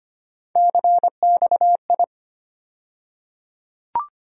Morse code, heard as a keyed tone, about 25 wpm, 700 Hz. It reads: CXI
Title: USA Call Sign Postfixes - 3 Letters 25wpm